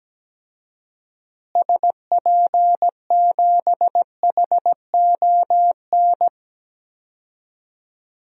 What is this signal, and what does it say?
Morse code, heard as a keyed tone, about 17 wpm, 700 Hz. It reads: SP7HON
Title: DXCC-100 Call Signs - 6 Characters - Repeated 2x - 17wpm